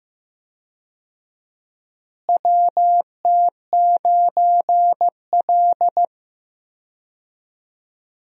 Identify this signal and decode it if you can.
Morse code, heard as a keyed tone, about 15 wpm, 700 Hz. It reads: WT9L
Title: USA Call Signs - 4 Characters - Repeated 3x - 15wpm